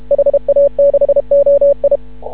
Morse code, heard as a keyed tone, about 31 wpm, 560 Hz. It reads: HA6OI
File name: CW / ha6oi 2